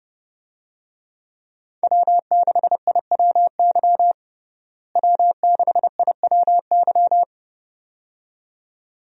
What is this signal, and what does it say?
Morse code, heard as a keyed tone, about 32 wpm, 700 Hz. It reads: W6IWY W6IWY
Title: USA Call Signs - 5 Characters - Repeated 2x - 30wpm